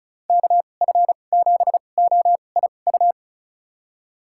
Morse code, heard as a keyed tone, about 36 wpm, spaced as for 25 wpm, 700 Hz. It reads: KF7OIU
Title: USA Call Signs - 6 Characters - 2x Character Spacing - 35wpm